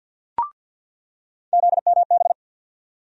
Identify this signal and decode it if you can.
Morse code, heard as a keyed tone, about 47 wpm, 700 Hz. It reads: ZMB
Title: USA Call Sign Postfixes - 3 Letters 50wpm